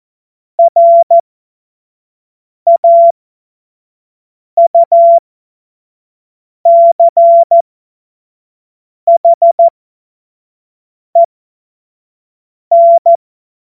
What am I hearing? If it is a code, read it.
Morse code, heard as a keyed tone, about 14 wpm, spaced as for 5 wpm, 680 Hz. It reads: RAUCHEN
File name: Morsekurs / Witze / Witz 02 April 2024 25 F70